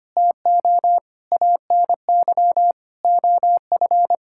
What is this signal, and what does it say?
Morse code, heard as a keyed tone, about 25 wpm, 700 Hz. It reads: TO ANY OF
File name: Sets of 9 Words - 25wpm